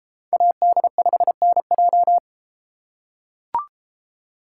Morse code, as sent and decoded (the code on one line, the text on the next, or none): .- -.. ..... -. .---
AD5NJ